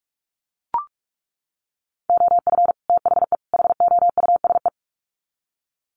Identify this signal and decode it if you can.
Morse code, heard as a keyed tone, about 44 wpm, 700 Hz. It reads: OF THE HOUSE
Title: Sets of 3 Words 45wpm